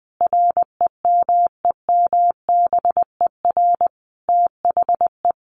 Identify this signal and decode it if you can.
Morse code, heard as a keyed tone, about 20 wpm, 700 Hz. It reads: REMEMBER THE
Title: Sentences from Top 800 Words 20wpm